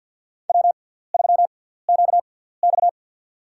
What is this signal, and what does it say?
Morse code, heard as a keyed tone, about 49 wpm, spaced as for 17 wpm, 700 Hz. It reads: W3QX
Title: USA Call Signs - 4 Characters - 6x Character Spacing - 50wpm